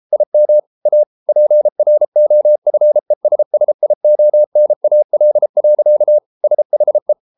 Morse code, heard as a keyed tone, about 33 wpm, 600 Hz. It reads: IM A PROFESSIONAL. SHE